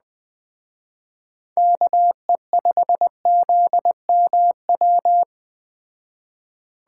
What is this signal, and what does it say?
Morse code, heard as a keyed tone, about 20 wpm, 700 Hz. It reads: KE5ZMW
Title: Speed Racing - USA 2x3 Call Signs 17wpm